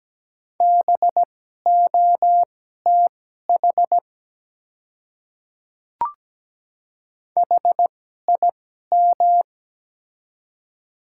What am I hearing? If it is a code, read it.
Morse code, heard as a keyed tone, about 17 wpm, spaced as for 12 wpm, 690 Hz. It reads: BOTH HIM
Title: 2x Character Spacing - Top 300 Words - 17wpm